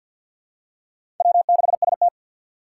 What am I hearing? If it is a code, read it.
Morse code, heard as a keyed tone, about 48 wpm, 690 Hz. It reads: W6ST
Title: USA Call Signs - 4 Characters - 6x Character Spacing - 50wpm